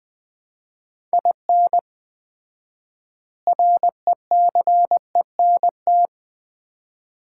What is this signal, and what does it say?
Morse code, heard as a keyed tone, about 20 wpm, 700 Hz. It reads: IN RECENT
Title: 4x Word-Spacing - Sets of 3 Words - 20wpm